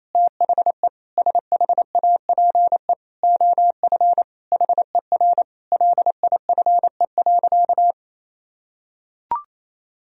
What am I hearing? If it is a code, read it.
Morse code, heard as a keyed tone, about 28 wpm, 700 Hz. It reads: THE SHAPE OF HER LIFE.
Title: Sentences from Top 1000 Words - Review Part 3 - 28wpm